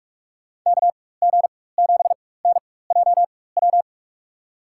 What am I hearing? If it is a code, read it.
Morse code, heard as a keyed tone, about 46 wpm, spaced as for 21 wpm, 700 Hz. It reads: KG7NJW
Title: USA Call Signs - 4x Character Spacing - 45wpm